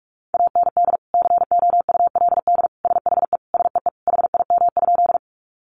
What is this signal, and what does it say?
Morse code, heard as a keyed tone, about 44 wpm, 700 Hz. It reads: AND COULD SHE SEE HIM?
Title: Sentences from Top 100 Words 45wpm